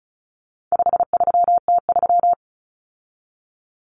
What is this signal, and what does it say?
Morse code, heard as a keyed tone, about 34 wpm, 700 Hz. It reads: 53T3
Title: Common Cut Numbers - 4-Digit 35wpm